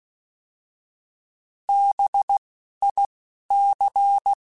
Morse code, heard as a keyed tone, about 16 wpm, spaced as for 11 wpm, 800 Hz. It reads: BIC
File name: Alphabetical / Alphabetical mp3s / Bs @10